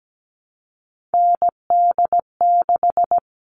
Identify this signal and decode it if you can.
Morse code, heard as a keyed tone, about 17 wpm, 700 Hz. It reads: ND6
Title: All Cut Numbers - 3-Digit 17wpm